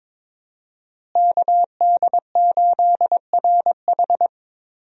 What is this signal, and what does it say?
Morse code, heard as a keyed tone, about 22 wpm, 700 Hz. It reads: KD8RH
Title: USA Call Signs - 5 Characters - Repeated 2x - 22wpm